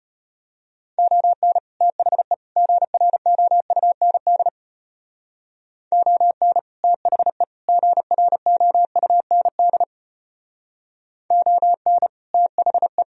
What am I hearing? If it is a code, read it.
Morse code, heard as a keyed tone, about 30 wpm, 700 Hz. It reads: ON THE GROUND ON THE GROUND ON THE
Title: Speed-Racing - Sets of 3 Words 25wpm